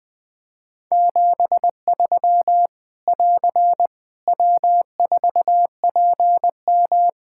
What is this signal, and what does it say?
Morse code, heard as a keyed tone, about 20 wpm, 700 Hz. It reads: 73 <AR> W4PM
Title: K1USN SST QSO Practice 20wpm